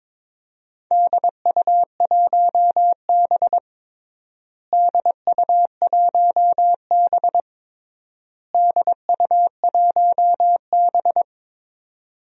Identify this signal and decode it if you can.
Morse code, heard as a keyed tone, about 22 wpm, 700 Hz. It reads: DU1B DU1B DU1B
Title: DXCC-100 Call Signs - 4 Characters - Repeated 3x - 22wpm